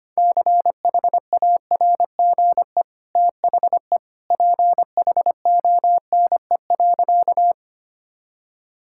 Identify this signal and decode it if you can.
Morse code, heard as a keyed tone, about 25 wpm, 700 Hz. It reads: CHARGE THE PHONE.
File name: Easy Sentences with Hints - 25wpm